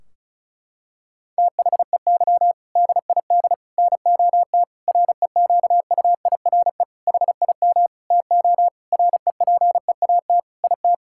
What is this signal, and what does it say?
Morse code, heard as a keyed tone, about 36 wpm, 700 Hz. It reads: THEY DID NOT REQUIRE HIM TO REPEAT IT
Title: Sentences from Top 1600 Words 35wpm